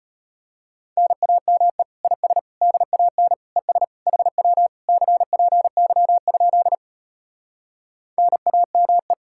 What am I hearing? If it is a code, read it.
Morse code, heard as a keyed tone, about 33 wpm, 700 Hz. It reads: NAME IS DAN ES HW CPY? NAME